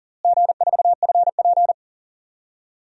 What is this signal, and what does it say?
Morse code, heard as a keyed tone, about 40 wpm, 700 Hz. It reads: G4FP
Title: DXCC-100 Call Signs - 4 Characters - Repeated 2x - 40wpm